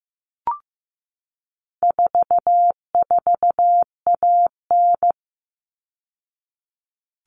Morse code, heard as a keyed tone, about 15 wpm, 700 Hz. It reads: 44AN
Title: All Cut Numbers - 4-Digit 15wpm